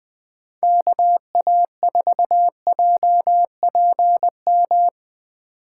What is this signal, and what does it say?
Morse code, heard as a keyed tone, about 19 wpm, 700 Hz. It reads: KA4JPM